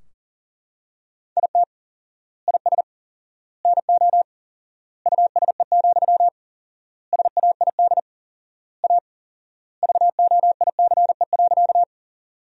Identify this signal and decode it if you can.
Morse code, heard as a keyed tone, about 40 wpm, 700 Hz. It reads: IT IS NO USE, SAID A VOICE.
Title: 4x Word-Spacing - Sentences from Top 500 Words - 40wpm